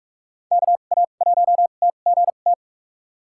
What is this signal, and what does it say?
Morse code, heard as a keyed tone, about 46 wpm, spaced as for 33 wpm, 700 Hz. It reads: KA1TGT
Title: USA Call Signs - 6 Characters - 2x Character Spacing - 45wpm